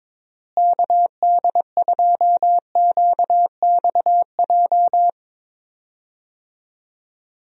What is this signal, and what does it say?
Morse code, heard as a keyed tone, about 21 wpm, 700 Hz. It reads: KD2QXJ
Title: USA Call Signs - 6 Characters - Repeated 2x - 22wpm